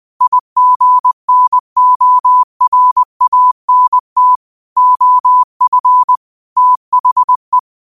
Morse code, heard as a keyed tone, about 20 wpm, 1000 Hz. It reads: IGNORANT OF THE